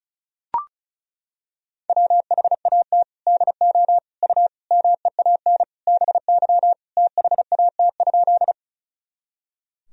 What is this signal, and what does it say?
Morse code, heard as a keyed tone, about 36 wpm, 700 Hz. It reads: WHAT DO U MEAN BY THAT?
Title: Sentences from Top 200 Words and Abbreviations 35wpm